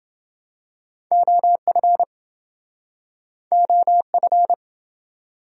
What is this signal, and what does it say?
Morse code, heard as a keyed tone, about 27 wpm, 700 Hz. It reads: OF OF